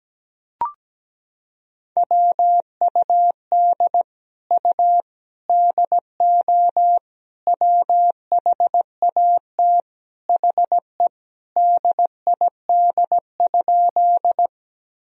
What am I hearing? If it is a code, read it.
Morse code, heard as a keyed tone, about 17 wpm, 700 Hz. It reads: WUD U DO WHAT HE DID?
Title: Sentences from Top 100 Words and Abbreviations 17wpm